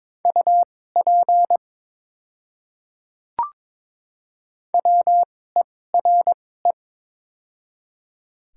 Morse code, heard as a keyed tone, about 22 wpm, spaced as for 16 wpm, 700 Hz. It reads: UP WERE
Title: 2x Character Spacing - Top 200 Words - 22wpm